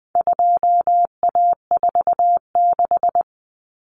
Morse code, heard as a keyed tone, about 20 wpm, 700 Hz. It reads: 2A46
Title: Common Cut Numbers - 4-Digit 20wpm